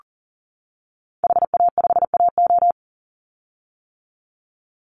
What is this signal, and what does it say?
Morse code, heard as a keyed tone, about 40 wpm, 700 Hz. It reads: HA5AO
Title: Call Signs - 40wpm